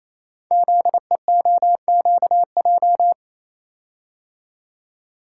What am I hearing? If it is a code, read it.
Morse code, heard as a keyed tone, about 28 wpm, 700 Hz. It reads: ZEOQJ
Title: Code Groups - 5 Letters - 28wpm